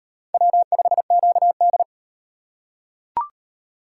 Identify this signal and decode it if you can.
Morse code, heard as a keyed tone, about 32 wpm, 700 Hz. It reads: W5QD